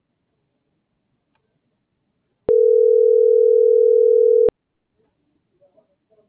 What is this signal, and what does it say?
Ringback tone (North America), 440 + 480 Hz.